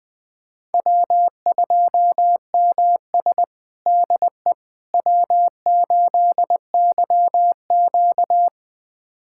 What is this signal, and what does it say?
Morse code, heard as a keyed tone, about 20 wpm, 700 Hz. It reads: W2MS DE W8YQ
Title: USA Callsign DE Callsign - 4 Characters - 20wpm